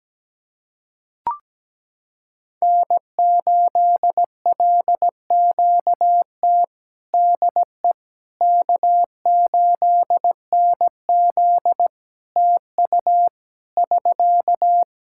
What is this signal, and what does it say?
Morse code, heard as a keyed tone, about 17 wpm, 700 Hz. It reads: N8LQT DE K8NZ TU <SK>
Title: Practice QSO - Exchange 4 - Ending - 17wpm